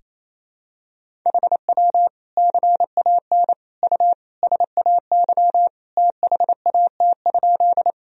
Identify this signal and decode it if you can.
Morse code, heard as a keyed tone, about 28 wpm, 700 Hz. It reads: HW CAN U SAY THAT?
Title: Sentences from Top 200 Words and Abbreviations 28wpm